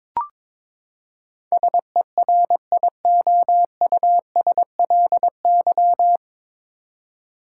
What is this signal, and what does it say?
Morse code, heard as a keyed tone, about 22 wpm, 700 Hz. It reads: SERIOUSLY
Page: Morse Code Ninja practice 22wpm